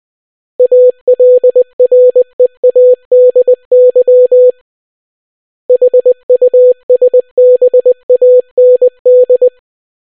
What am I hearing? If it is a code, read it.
Morse code, heard as a keyed tone, about 20 wpm, 500 Hz. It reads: ALREADY HUSBAND